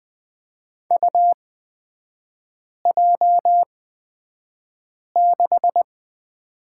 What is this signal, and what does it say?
Morse code, heard as a keyed tone, about 20 wpm, spaced as for 5 wpm, 690 Hz. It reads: UJ6
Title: 229 Lesson 37e - Letter x - Three Characters 20wpm